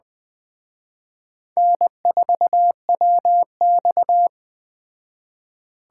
Morse code, heard as a keyed tone, about 21 wpm, 700 Hz. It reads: N4WX
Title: Speed Racing - USA Call Signs 17wpm